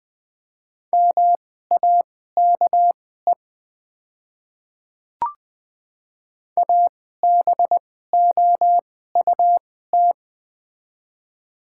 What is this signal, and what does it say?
Morse code, heard as a keyed tone, about 20 wpm, spaced as for 15 wpm, 700 Hz. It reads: MAKE ABOUT
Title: 2x Character Spacing - Top 200 Words - 20wpm